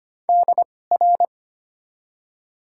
Morse code, heard as a keyed tone, about 25 wpm, spaced as for 18 wpm, 700 Hz. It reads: DR
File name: Top 90 QSO Elements - 2x Character Spacing - 25wpm